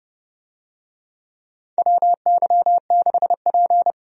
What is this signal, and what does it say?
Morse code, heard as a keyed tone, about 30 wpm, 700 Hz. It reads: WY6P
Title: USA Call Signs - 4 Characters - 30wpm